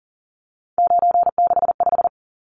Morse code, heard as a keyed tone, about 40 wpm, 700 Hz. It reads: 965
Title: Numbers - 3-Digit 40wpm